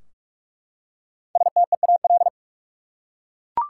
Morse code, heard as a keyed tone, about 47 wpm, 700 Hz. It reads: STEAL
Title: Five Letter Words - 45wpm